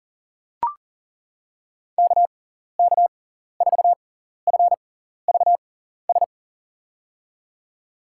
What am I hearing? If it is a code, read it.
Morse code, heard as a keyed tone, about 40 wpm, spaced as for 14 wpm, 700 Hz. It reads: KK4FVS
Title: USA Call Signs - 6 Characters - 6x Character Spacing - 40wpm